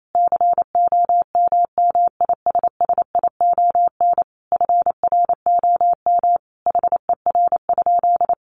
Morse code, heard as a keyed tone, about 28 wpm, 700 Hz. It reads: COMMISSION FROM HER?
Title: Sentences from Top 1400 Words 28wpm